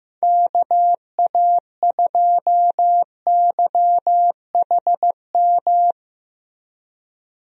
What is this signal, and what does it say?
Morse code, heard as a keyed tone, about 15 wpm, 700 Hz. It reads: KA2YHM
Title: USA Call Signs - 6 Characters - Repeated 3x - 15wpm